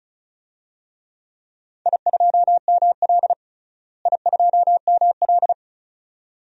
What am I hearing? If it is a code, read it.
Morse code, heard as a keyed tone, about 34 wpm, 700 Hz. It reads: I2ML I2ML
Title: DXCC-100 Call Signs - 4 Characters - Repeated 2x - 35wpm